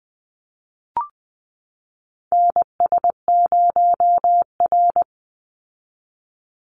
Morse code, heard as a keyed tone, about 20 wpm, 700 Hz. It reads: NS0R
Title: Call Signs - 20wpm